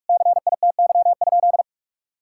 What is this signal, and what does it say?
Morse code, heard as a keyed tone, about 46 wpm, 700 Hz. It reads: CITY?
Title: Sentences from Top 1000 Words - Review Part 5 - 45wpm